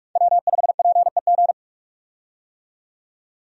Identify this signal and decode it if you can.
Morse code, heard as a keyed tone, about 42 wpm, 700 Hz. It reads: W5PEG
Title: USA Call Signs - 5 Characters - 45wpm